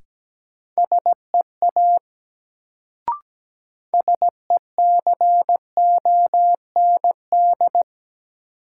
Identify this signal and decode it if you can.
Morse code, heard as a keyed tone, about 17 wpm, 700 Hz. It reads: SEA SECOND
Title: Sentences from Top 300 Words - New Word Drill - 17wpm